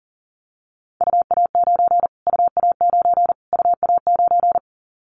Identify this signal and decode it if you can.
Morse code, heard as a keyed tone, about 40 wpm, 700 Hz. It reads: UA9 UA9 UA9
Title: Callsign Prefixes for Top 25 DXCC - Repeated 3x - 40wpm